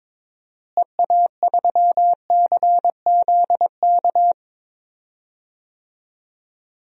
Morse code, heard as a keyed tone, about 22 wpm, 700 Hz. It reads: EA3CZK
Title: DXCC-100 Call Signs - 6 Characters - Repeated 2x - 22wpm